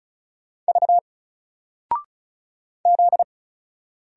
Morse code, heard as a keyed tone, about 36 wpm, 700 Hz. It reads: V Z
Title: Single Character (Letter:Number:Punctuation:Prosign) Alt 35wpm